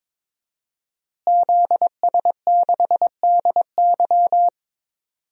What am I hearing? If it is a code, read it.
Morse code, heard as a keyed tone, about 22 wpm, 700 Hz. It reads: ZS6DY
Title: DXCC-100 Call Signs - 5 Characters - Repeated 2x - 22wpm